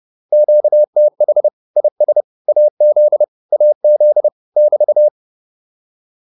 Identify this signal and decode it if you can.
Morse code, heard as a keyed tone, about 30 wpm, 600 Hz. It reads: QTH IS AZ AZ <BT>